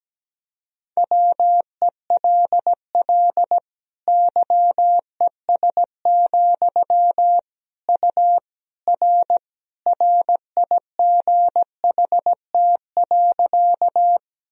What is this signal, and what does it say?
Morse code, heard as a keyed tone, about 17 wpm, 700 Hz. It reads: WELL YES, U R RIGHT.
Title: Sentences from Top 500 Words and Abbreviations 17wpm